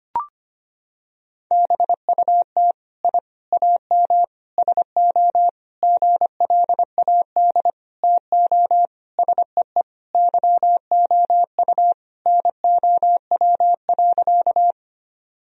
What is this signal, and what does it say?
Morse code, heard as a keyed tone, about 25 wpm, 700 Hz. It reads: BUT I AM SO GLAD TO SEE YOU NOW.
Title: Sentences from Top 1000 Words - Review Part 2 - 25wpm